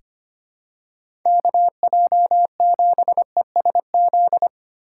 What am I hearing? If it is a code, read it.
Morse code, heard as a keyed tone, about 25 wpm, 700 Hz. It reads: KJ7ESZ
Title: USA Call Signs - 6 Characters - 3x Character Spacing - 25wpm